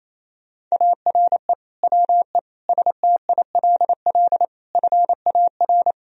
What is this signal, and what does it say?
Morse code, heard as a keyed tone, about 28 wpm, 700 Hz. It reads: ARE WE STILL FAR
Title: Sentences from Top 1000 Words - Review Part 4 - 28wpm